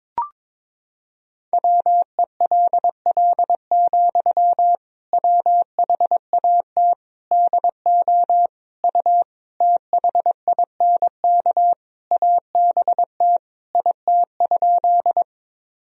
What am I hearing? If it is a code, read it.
Morse code, heard as a keyed tone, about 22 wpm, 700 Hz. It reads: WELL, WHAT DO U THINK ABT IT?
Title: Sentences from Top 200 Words and Abbreviations 22wpm